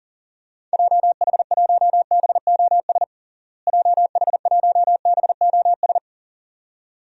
Morse code, heard as a keyed tone, about 40 wpm, 700 Hz. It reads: JH1BOS JH1BOS
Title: DXCC-100 Call Signs - 6 Characters - Repeated 2x - 40wpm